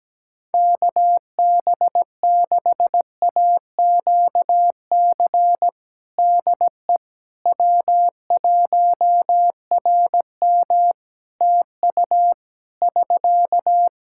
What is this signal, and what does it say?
Morse code, heard as a keyed tone, about 17 wpm, 700 Hz. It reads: KB6AQC DE W1RM TU <SK>
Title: Practice QSO - Exchange 4 - Ending - 17wpm